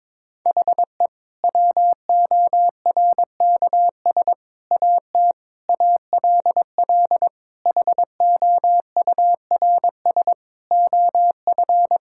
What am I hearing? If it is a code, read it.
Morse code, heard as a keyed tone, about 22 wpm, 700 Hz. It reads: HE WORKS AT ALL HOURS OF
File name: practice / 22wpm / Sentences with 8-Word Phrases - 22wpm